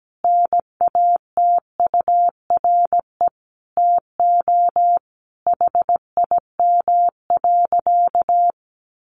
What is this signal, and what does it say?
Morse code, heard as a keyed tone, about 17 wpm, 700 Hz. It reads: NATURE TO HIM.
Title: Sentences from Top 600 Words 17wpm